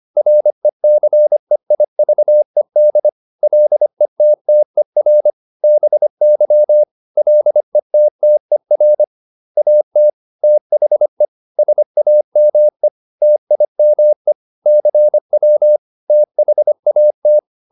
Morse code, heard as a keyed tone, about 25 wpm, 600 Hz. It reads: RECEIVED LETTER BY LETTER AT THE SAME TIME CW THAT